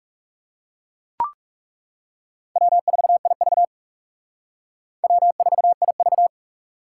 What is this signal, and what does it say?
Morse code, heard as a keyed tone, about 39 wpm, 700 Hz. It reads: W4IV W4IV